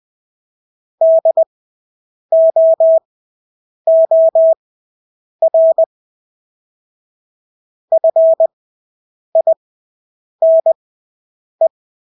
Morse code, heard as a keyed tone, about 20 wpm, spaced as for 8 wpm, 650 Hz. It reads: DOOR FINE
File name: Engelske ord 20/8 wpm